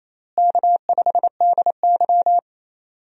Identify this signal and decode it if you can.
Morse code, heard as a keyed tone, about 28 wpm, 700 Hz. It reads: K5DY
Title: USA Call Signs - 4 Characters - Repeated 3x - 28wpm